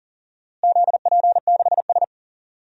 Morse code, heard as a keyed tone, about 40 wpm, 700 Hz. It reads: ZP6S
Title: DXCC-100 Call Signs - 4 Characters - Repeated 3x - 40wpm